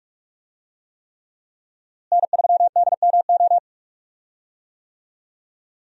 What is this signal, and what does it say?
Morse code, heard as a keyed tone, about 41 wpm, 700 Hz. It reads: N3DMO
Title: USA Call Signs - 5 Characters - Repeated 3x - 45wpm